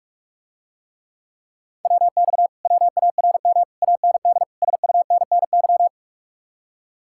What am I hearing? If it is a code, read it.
Morse code, heard as a keyed tone, about 49 wpm, 700 Hz. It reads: WX WARM AND SUNNY